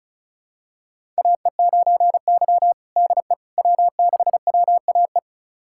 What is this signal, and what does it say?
Morse code, heard as a keyed tone, about 34 wpm, 700 Hz. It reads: AE9Y DE W6WAE
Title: USA Callsign DE Callsign 35wpm